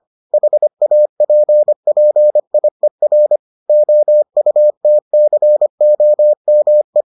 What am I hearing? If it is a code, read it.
Morse code, heard as a keyed tone, about 25 wpm, 600 Hz. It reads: HAPPIER OUTCOME